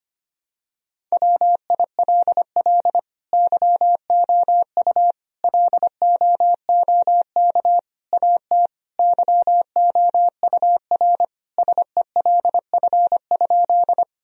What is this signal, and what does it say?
Morse code, heard as a keyed tone, about 25 wpm, 700 Hz. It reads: WILL YOU LOOK AT YOUR SELF?
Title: Sentences from Top 1000 Words - Review Part 1 - 25wpm